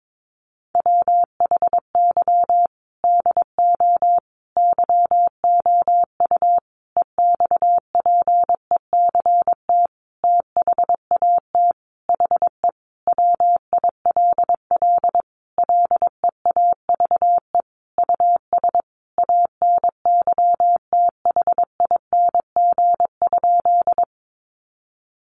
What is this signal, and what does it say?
Morse code, heard as a keyed tone, about 22 wpm, 700 Hz. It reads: WHY DO YOU EXPECT THAT HE WILL LEAVE US ANYTHING?
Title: Sentences from Top 1100 Words 22wpm